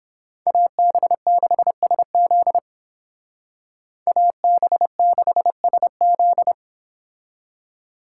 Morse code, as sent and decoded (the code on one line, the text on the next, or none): .- -... -.... ... --.. / .- -... -.... ... --..
AB6SZ AB6SZ